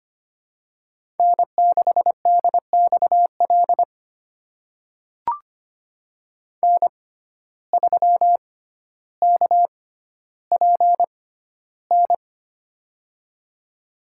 Morse code, heard as a keyed tone, about 25 wpm, spaced as for 13 wpm, 700 Hz. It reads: N6DXL N3KPN
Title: USA Call Signs - 6x Character Spacing - 25wpm